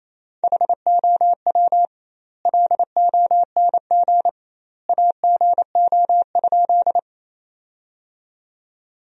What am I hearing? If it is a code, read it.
Morse code, heard as a keyed tone, about 28 wpm, 700 Hz. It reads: HOW LONG AGO?
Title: 2x Word-Spacing - Sentences from Top 500 Words - 28wpm